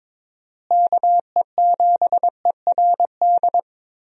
Morse code, heard as a keyed tone, about 22 wpm, 700 Hz. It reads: KE7ERD
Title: USA Call Signs - 6 Characters - Repeated 2x - 22wpm